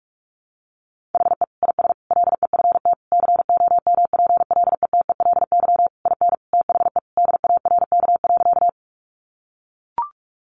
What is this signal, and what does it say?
Morse code, heard as a keyed tone, about 44 wpm, 700 Hz. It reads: HE IS LEFT COMPLETELY IN THE DARK.
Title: Sentences from Top 1100 Words 45wpm